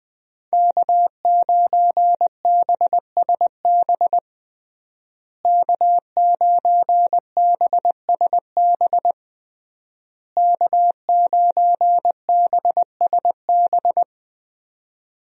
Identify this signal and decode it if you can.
Morse code, heard as a keyed tone, about 20 wpm, 700 Hz. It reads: K9BSB K9BSB K9BSB
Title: USA Call Signs - 5 Characters - Repeated 3x - 20wpm